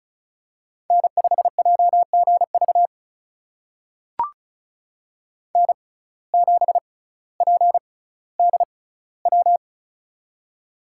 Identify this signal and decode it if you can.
Morse code, heard as a keyed tone, about 36 wpm, spaced as for 18 wpm, 700 Hz. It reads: N5JGV N7PDW
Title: USA Call Signs - 5 Characters - 6x Character Spacing - 35wpm